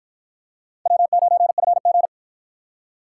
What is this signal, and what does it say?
Morse code, heard as a keyed tone, about 53 wpm, 700 Hz. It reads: W9FG